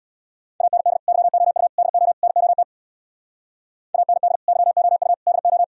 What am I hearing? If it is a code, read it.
Morse code, heard as a keyed tone, about 64 wpm, 690 Hz. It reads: AND YOU DO NOT AND YOU DO